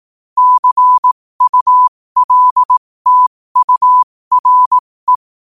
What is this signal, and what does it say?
Morse code, heard as a keyed tone, about 18 wpm, spaced as for 15 wpm, 1000 Hz. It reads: CULTURE
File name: qotdfiles / 15wpm